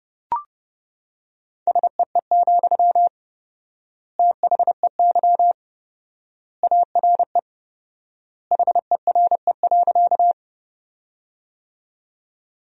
Morse code, heard as a keyed tone, about 30 wpm, 700 Hz. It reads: SEE, THEY ARE HERE.
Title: 4x Word-Spacing - Sentences from Top 200 Words - 30wpm